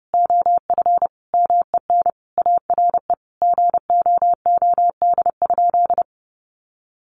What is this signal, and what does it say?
Morse code, heard as a keyed tone, about 30 wpm, 700 Hz. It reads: OF MEN ARE GOOD?
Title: Sentences from Top 1300 Words 30wpm